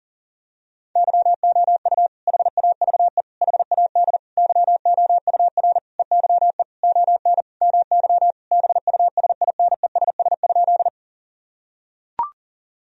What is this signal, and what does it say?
Morse code, heard as a keyed tone, about 40 wpm, 700 Hz. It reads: YOU HAVE HAD YOUR EYE ON MY BUSINESS?
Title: Sentences from Top 1000 Words - Review Part 1 - 40wpm